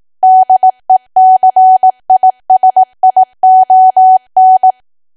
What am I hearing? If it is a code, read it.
Morse code, heard as a keyed tone, about 18 wpm, 750 Hz. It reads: DECISION
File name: cwa / W1AW-Level2 / 18WPM